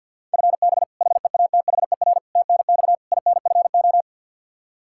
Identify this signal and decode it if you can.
Morse code, heard as a keyed tone, about 51 wpm, 700 Hz. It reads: FB HEATHER TNX INFO
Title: Practice QSO - Exchange 3 - More Info - 50wpm